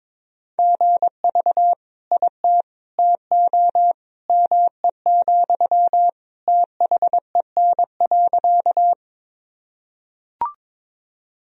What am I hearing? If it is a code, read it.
Morse code, heard as a keyed tone, about 22 wpm, 700 Hz. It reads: GV IT TO ME, THEN.
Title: Sentences from Top 200 Words and Abbreviations 22wpm